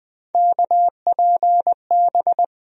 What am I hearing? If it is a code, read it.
Morse code, heard as a keyed tone, about 20 wpm, 700 Hz. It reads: KPB
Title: USA Call Sign Postfixes - 3 Letters 20wpm